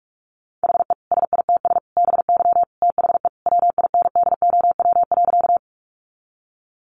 Morse code, heard as a keyed tone, about 44 wpm, 700 Hz. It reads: HE SITS BY THE WINDOW.
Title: Additional Easy Phrases 45wpm